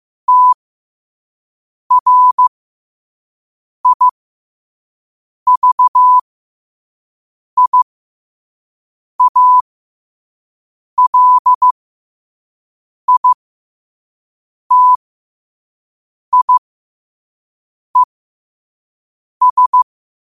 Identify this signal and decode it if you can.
Morse code, heard as a keyed tone, about 15 wpm, spaced as for 5 wpm, 1000 Hz. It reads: TRIVIALITIES
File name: qotdfiles / 5wpm